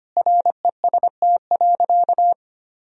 Morse code, heard as a keyed tone, about 25 wpm, 700 Hz. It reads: REST.
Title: Sentences from Top 500 Words and Abbreviations 25wpm